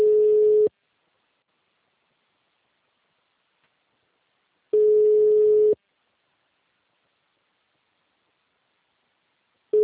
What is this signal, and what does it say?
Ringback tone (Europe, 425 Hz).